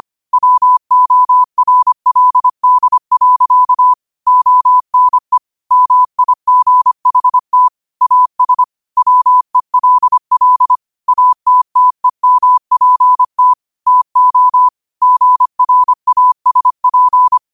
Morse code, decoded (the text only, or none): WORLD. ONE MIGHT AS WELL ATTEMPT TO GRASP